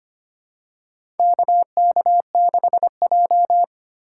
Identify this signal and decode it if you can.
Morse code, heard as a keyed tone, about 25 wpm, 700 Hz. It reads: KK6J
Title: USA Call Signs - 4 Characters - 4x Character Spacing - 25wpm